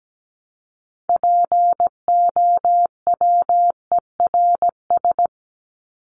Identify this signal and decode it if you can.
Morse code, heard as a keyed tone, about 17 wpm, 700 Hz. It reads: POWERS
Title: ES:S Suffix - Top 100 Words with Base Words 17wpm